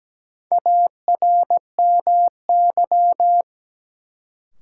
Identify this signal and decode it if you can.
Morse code, heard as a keyed tone, about 17 wpm, 700 Hz. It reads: ARMY
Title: Sentences from Top 800 Words - New Word Drill - 17wpm